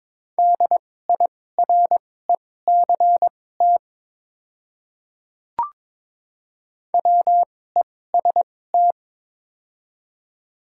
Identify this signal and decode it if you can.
Morse code, heard as a keyed tone, about 22 wpm, spaced as for 16 wpm, 700 Hz. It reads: DIRECT WEST
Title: 2x Character Spacing - Top 500 Words - 22wpm